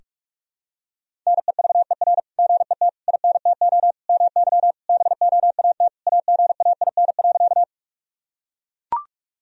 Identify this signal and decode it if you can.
Morse code, heard as a keyed tone, about 43 wpm, 700 Hz. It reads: NEVER GET INTO MY BOAT AGAIN.
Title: Sentences from Top 1000 Words - Review Part 3 - 45wpm